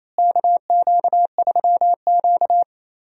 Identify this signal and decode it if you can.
Morse code, heard as a keyed tone, about 28 wpm, 700 Hz. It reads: KQ3Q
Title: USA Call Signs - 4 Characters - Repeated 3x - 28wpm